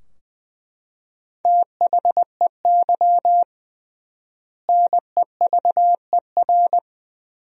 Morse code, heard as a keyed tone, about 20 wpm, 700 Hz. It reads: THEY NEVER
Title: 3x Word-Spacing - Sentences from Top 300 Words - 20wpm